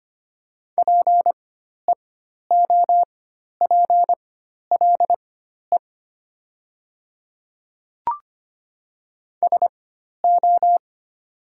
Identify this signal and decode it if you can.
Morse code, heard as a keyed tone, about 25 wpm, spaced as for 12 wpm, 700 Hz. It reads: PEOPLE SO
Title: 4x Character Spacing - Top 100 Words - 25wpm